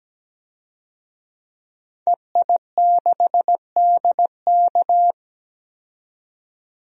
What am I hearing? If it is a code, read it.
Morse code, heard as a keyed tone, about 17 wpm, 700 Hz. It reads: EI6DK
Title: DXCC-100 Call Signs - 5 Characters - 17wpm